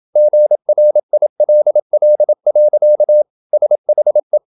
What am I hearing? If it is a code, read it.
Morse code, heard as a keyed tone, about 28 wpm, 600 Hz. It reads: GRILL. SHE
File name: Never 0027wpm